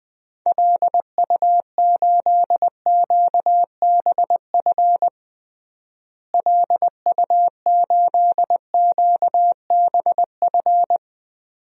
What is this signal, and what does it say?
Morse code, heard as a keyed tone, about 21 wpm, 700 Hz. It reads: LU8QBF LU8QBF
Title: DXCC-100 Call Signs - 6 Characters - Repeated 2x - 20wpm